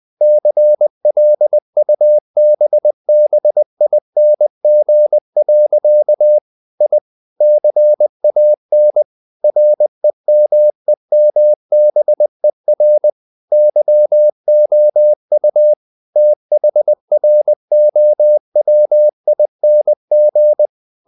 Morse code, heard as a keyed tone, about 20 wpm, 600 Hz. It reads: CLUBBING. I CAN REMEMBER YOU THROWING
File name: the old man and the sea / the old man and the sea part 1 20wpm